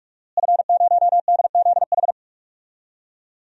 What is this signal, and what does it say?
Morse code, heard as a keyed tone, about 42 wpm, 700 Hz. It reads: F0DZH
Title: DXCC-100 Call Signs - 5 Characters - Repeated 2x - 45wpm